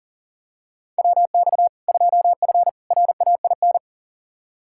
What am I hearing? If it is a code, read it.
Morse code, heard as a keyed tone, about 40 wpm, 700 Hz. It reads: WX 2F RAIN
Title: K1USN SST QSO Practice 40wpm